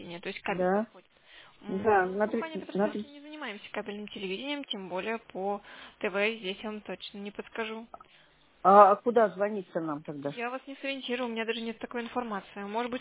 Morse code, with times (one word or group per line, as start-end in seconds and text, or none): none